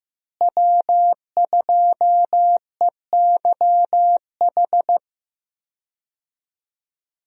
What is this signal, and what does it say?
Morse code, heard as a keyed tone, about 15 wpm, 700 Hz. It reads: W2EYH
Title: USA Call Signs - 5 Characters - Repeated 2x - 15wpm